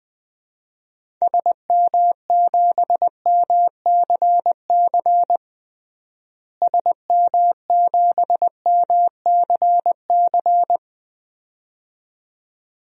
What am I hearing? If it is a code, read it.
Morse code, heard as a keyed tone, about 20 wpm, 700 Hz. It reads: SM7MCC SM7MCC
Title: DXCC-100 Call Signs - 6 Characters - Repeated 2x - 20wpm